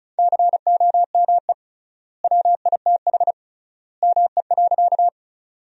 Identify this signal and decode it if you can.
Morse code, heard as a keyed tone, about 34 wpm, 700 Hz. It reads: COME WITH ME.
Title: 3x Word-Spacing - Sentences from Top 200 Words - 35wpm